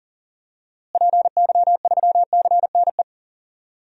Morse code, heard as a keyed tone, about 40 wpm, 700 Hz. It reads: PY3CNE
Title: DXCC-100 Call Signs - 6 Characters - Repeated 3x - 40wpm